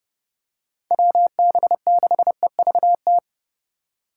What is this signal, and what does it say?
Morse code, heard as a keyed tone, about 29 wpm, 700 Hz. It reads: WB6EVT